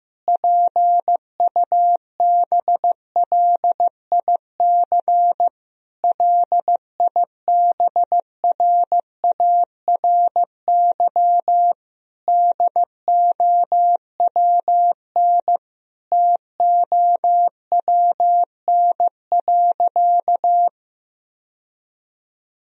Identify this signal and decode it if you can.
Morse code, heard as a keyed tone, about 15 wpm, 700 Hz. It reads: PUBLIC LIBRARY DOWN TOWN.
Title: Sentences from Top 1000 Words - Review Part 4 - 15wpm